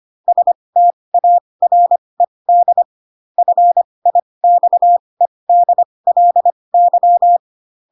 Morse code, heard as a keyed tone, about 25 wpm, spaced as for 20 wpm, 700 Hz. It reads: STARED FIXEDLY
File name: Pinocchio / Pinocchio ch3 15 25 ik1hge